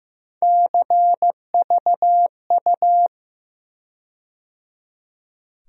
Morse code, heard as a keyed tone, about 15 wpm, 700 Hz. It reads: CVU
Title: USA Call Sign Postfixes - 3 Letters 15wpm